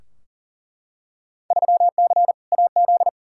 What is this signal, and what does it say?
Morse code, heard as a keyed tone, about 40 wpm, 700 Hz. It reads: 3C AZ
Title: Field Day Exchanges Encore - 40wpm